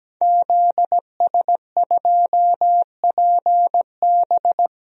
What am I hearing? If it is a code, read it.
Morse code, heard as a keyed tone, about 17 wpm, 700 Hz. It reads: ZS2PB
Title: DXCC-100 Call Signs - 5 Characters - Repeated 2x - 17wpm